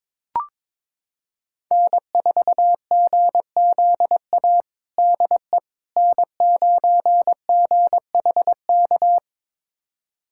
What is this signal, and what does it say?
Morse code, heard as a keyed tone, about 22 wpm, 700 Hz. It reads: N4GZA DE N9GHK
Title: USA Callsign DE Callsign - 5 Characters - 22wpm